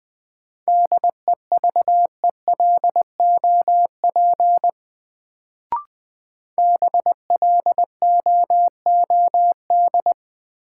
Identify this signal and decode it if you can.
Morse code, heard as a keyed tone, about 20 wpm, 700 Hz. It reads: DEVELOP BLOOD
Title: Sentences from Top 1000 Words - Review Part 2 - New Word Drill - 20wpm